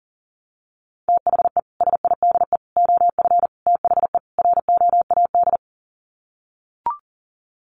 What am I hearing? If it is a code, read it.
Morse code, heard as a keyed tone, about 40 wpm, 700 Hz. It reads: THE SIDE OF THE ROAD
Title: Sets of 5 Words 40wpm